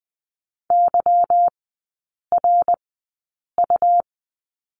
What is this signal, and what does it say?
Morse code, heard as a keyed tone, about 20 wpm, spaced as for 8 wpm, 690 Hz. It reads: YRU
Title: 145 Lesson 24e - Letter y - Three Characters 20wpm